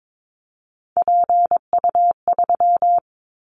Morse code, heard as a keyed tone, about 22 wpm, 700 Hz. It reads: PU3
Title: Callsign Prefixes for Top 100 DXCC - 22wpm